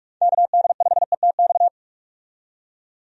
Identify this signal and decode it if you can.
Morse code, heard as a keyed tone, about 45 wpm, 700 Hz. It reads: KD5ETX
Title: USA Call Signs - 6 Characters - 45wpm